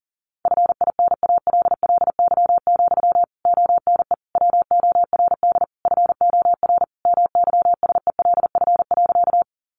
Morse code, heard as a keyed tone, about 40 wpm, 700 Hz. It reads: FINALLY, ONE WORD FOR MYSELF.
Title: Sentences from Top 1300 Words 40wpm